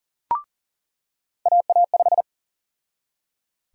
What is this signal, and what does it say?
Morse code, heard as a keyed tone, about 40 wpm, 700 Hz. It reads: AA5
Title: USA Call Sign Prefixes - 40wpm